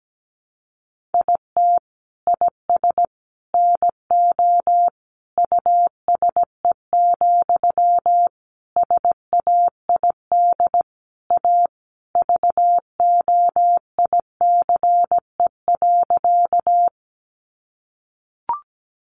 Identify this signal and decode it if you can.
Morse code, heard as a keyed tone, about 17 wpm, 700 Hz. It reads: IT IS NO USE, SAID A VOICE.
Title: Sentences from Top 500 Words 17wpm